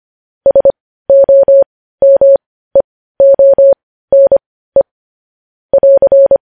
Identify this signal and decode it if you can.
Morse code, heard as a keyed tone, about 25 wpm, spaced as for 15 wpm, 550 Hz. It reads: SOMEONE <AR>
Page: news-headlines-in-morse-code-at-15-wpm